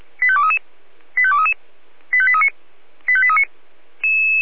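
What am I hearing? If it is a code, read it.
Five-tone selective call (ZVEI), heard as quick runs of five tones, tones of about 70 ms. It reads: 85233 85233 85839 85839
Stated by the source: ZVEI five-tone selective call received on an SDR.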